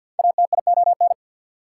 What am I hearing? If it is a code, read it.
Morse code, heard as a keyed tone, about 52 wpm, 690 Hz. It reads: ATION
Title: Prefixes and Suffixes 50wpm